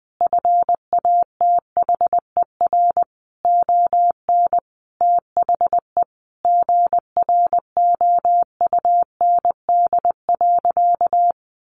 Morse code, heard as a keyed tone, about 20 wpm, 700 Hz. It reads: FATHER ON THE GROUND.
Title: Sentences from Top 500 Words 20wpm